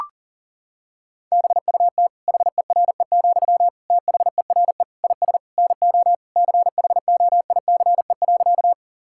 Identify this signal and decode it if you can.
Morse code, heard as a keyed tone, about 37 wpm, 700 Hz. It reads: BUT HERE, THERE IS NO CHOICE.
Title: Sentences from Top 1000 Words - Review Part 6 - 40wpm